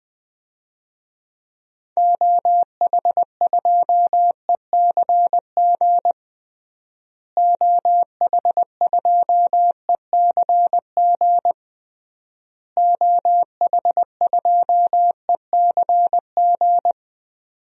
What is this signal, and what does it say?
Morse code, heard as a keyed tone, about 20 wpm, 700 Hz. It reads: OH2ECG OH2ECG OH2ECG
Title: DXCC-100 Call Signs - 6 Characters - Repeated 3x - 20wpm